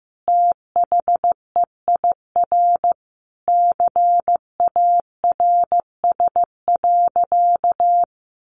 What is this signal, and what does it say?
Morse code, heard as a keyed tone, about 15 wpm, 700 Hz. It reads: THEIR CARS.
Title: Easy Phrases 15wpm